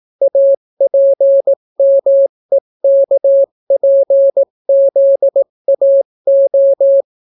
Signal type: Morse code